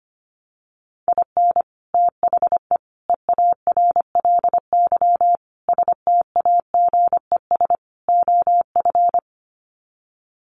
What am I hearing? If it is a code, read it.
Morse code, heard as a keyed tone, about 25 wpm, 700 Hz. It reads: IN THE EARLY STAGES OF
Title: Sets of 5 Words Encore 25wpm